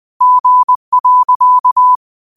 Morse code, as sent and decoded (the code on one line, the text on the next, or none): --. .-.-.-
G.